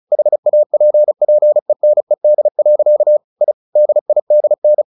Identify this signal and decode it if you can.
Morse code, heard as a keyed tone, about 35 wpm, 600 Hz. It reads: HAPPENED. I DIDN
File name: the old man and the sea 2 35wpm